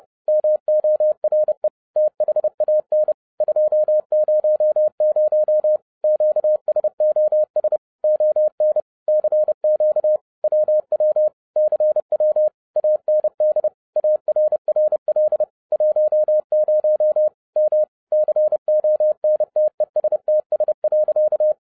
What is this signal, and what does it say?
Morse code, heard as a keyed tone, about 30 wpm, 610 Hz. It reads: MORE THAN 200 QSOS ON CQ WW CW AND ARRL 10 M CONTESTS.
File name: 2012-01-mp3 / n12-I DIDNT MISS THE SOLAR CYCLE 24-30wpm